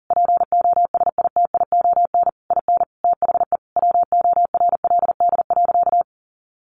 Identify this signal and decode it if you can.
Morse code, heard as a keyed tone, about 40 wpm, 700 Hz. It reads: POSITION IN THE WORLD.
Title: Sentences from Top 1300 Words 40wpm